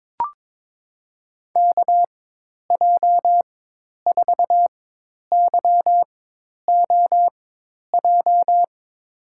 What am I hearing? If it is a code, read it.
Morse code, heard as a keyed tone, about 22 wpm, spaced as for 10 wpm, 700 Hz. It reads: KJ4YOJ
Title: USA Call Signs - 6 Characters - 4x Character Spacing - 22wpm